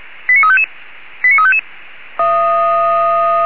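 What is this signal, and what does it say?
Five-tone selective call (ZVEI), heard as quick runs of five tones, tones of about 70 ms. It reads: 89355 89355
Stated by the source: ZVEI five-tone selective call received with an SDR.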